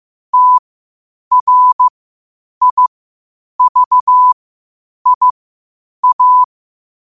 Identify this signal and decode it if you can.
Morse code, heard as a keyed tone, about 15 wpm, spaced as for 8 wpm, 1000 Hz. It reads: TRIVIA
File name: qotdfiles / 8wpm